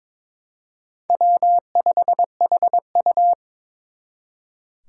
Morse code, heard as a keyed tone, about 26 wpm, 700 Hz. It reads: W5HU
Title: Speed Racing - USA 1x2 Call Signs 22wpm